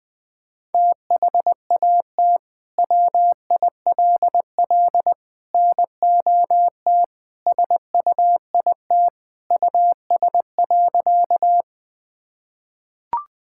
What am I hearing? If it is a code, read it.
Morse code, heard as a keyed tone, about 21 wpm, 700 Hz. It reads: THAT WILL NOT SUIT US.
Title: Sentences from Top 1600 Words 20wpm